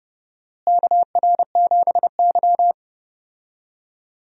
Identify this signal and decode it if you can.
Morse code, heard as a keyed tone, about 32 wpm, 700 Hz. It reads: KR7Y
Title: USA Call Signs - 4 Characters - Repeated 3x - 30wpm